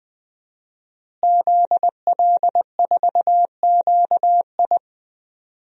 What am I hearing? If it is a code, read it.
Morse code, heard as a keyed tone, about 20 wpm, 700 Hz. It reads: ZL4QI
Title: DXCC-100 Call Signs - 5 Characters - Repeated 3x - 20wpm